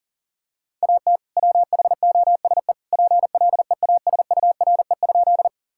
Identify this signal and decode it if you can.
Morse code, heard as a keyed tone, about 40 wpm, 700 Hz. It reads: AT WHOSE PLEASURE?
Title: Sentences from Top 1600 Words 40wpm